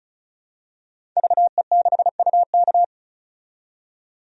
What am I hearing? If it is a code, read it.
Morse code, heard as a keyed tone, about 36 wpm, 700 Hz. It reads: VE6UK
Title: DXCC-100 Call Signs - 5 Characters - 35wpm